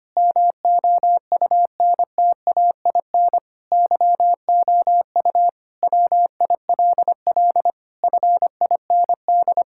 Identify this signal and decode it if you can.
Morse code, heard as a keyed tone, about 25 wpm, 700 Hz. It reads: MOUNTAIN YOU WILL FIND
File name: practice / 25wpm / Sentences with 3-Word Phrases - 25wpm